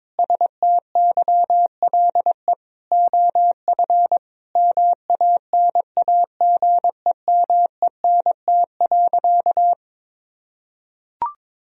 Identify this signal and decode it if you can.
Morse code, heard as a keyed tone, about 21 wpm, 700 Hz. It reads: STYLE OF MANAGEMENT.
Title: Sentences from Top 1000 Words - Review Part 4 - 22wpm